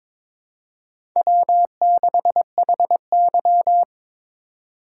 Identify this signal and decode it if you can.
Morse code, heard as a keyed tone, about 22 wpm, 700 Hz. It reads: W6HY
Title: USA Call Signs - 4 Characters - 2x Character Spacing - 22wpm